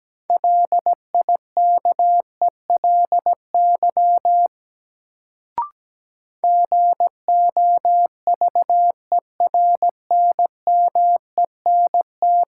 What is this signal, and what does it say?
Morse code, heard as a keyed tone, about 17 wpm, 700 Hz. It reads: LIKELY GOVERNMENT
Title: Sentences from Top 600 Words - New Word Drill - 17wpm